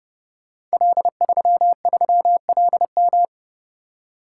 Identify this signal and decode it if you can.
Morse code, heard as a keyed tone, about 30 wpm, 700 Hz. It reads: L33LM
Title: Code Groups - 5 Letter:Numbers - 30wpm